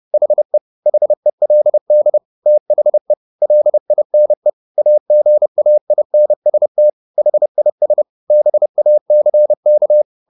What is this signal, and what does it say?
Morse code, heard as a keyed tone, about 30 wpm, 600 Hz. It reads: HE HELD THE LINE AGAINST HIS BACK
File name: the old man and the sea / the old man and the sea 1 30wpm